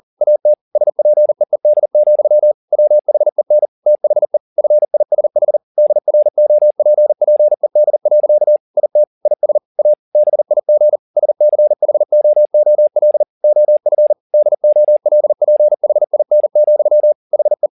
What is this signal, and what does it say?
Morse code, heard as a keyed tone, about 40 wpm, 600 Hz. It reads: AT SPEED, WHEN THE FISH DROPPED. IT IS A BIG SCHOOL OF DOLPHIN, HE